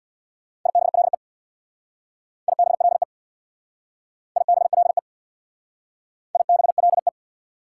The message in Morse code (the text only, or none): IBLE IBLE IBLE IBLE